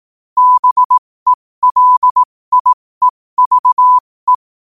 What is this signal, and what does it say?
Morse code, heard as a keyed tone, about 18 wpm, spaced as for 15 wpm, 1000 Hz. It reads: BELIEVE
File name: qotdfiles / 15wpm